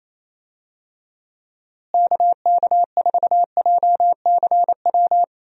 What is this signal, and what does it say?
Morse code, heard as a keyed tone, about 27 wpm, 700 Hz. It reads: KK4JCW
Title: USA Call Signs - 6 Characters - Repeated 2x - 28wpm